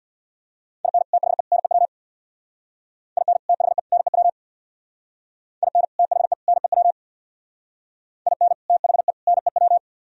Morse code, heard as a keyed tone, about 64 wpm, 700 Hz. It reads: IN THE NEW IN THE NEW IN THE NEW IN THE NEW